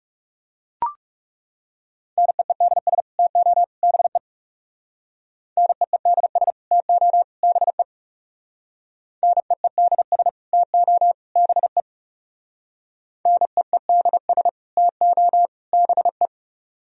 Morse code, heard as a keyed tone, about 37 wpm, 700 Hz. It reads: NEEDS TO BE NEEDS TO BE NEEDS TO BE NEEDS TO BE
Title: Speed-Racing - Sets of 3 Words 30wpm